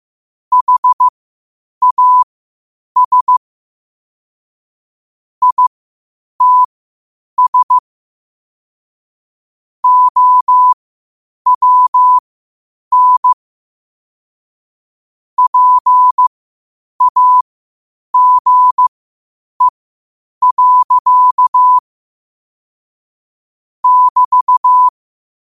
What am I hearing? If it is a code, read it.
Morse code, heard as a keyed tone, about 15 wpm, spaced as for 8 wpm, 1000 Hz. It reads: HAS ITS OWN PAGE. <BT>
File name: qotdfiles / 8wpm